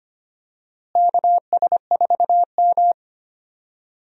Morse code, heard as a keyed tone, about 25 wpm, 700 Hz. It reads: KS4M
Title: USA Call Signs - 4 Characters - Repeated 3x - 25wpm